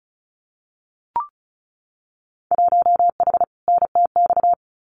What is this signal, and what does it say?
Morse code, heard as a keyed tone, about 34 wpm, 700 Hz. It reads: 1H NTX
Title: Winter Field Day Exchanges 35wpm